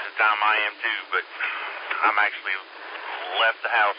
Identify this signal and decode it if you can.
Emergency Alert System (SAME) tones, heard as harsh buzzing bursts, with speech. It is heard from 1.0 to 3.8 s.